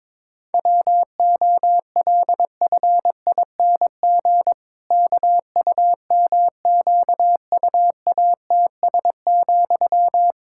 Morse code, heard as a keyed tone, about 22 wpm, 700 Hz. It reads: WOLFING KUMQUATS,
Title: Pangram Challenge v2 - 22wpm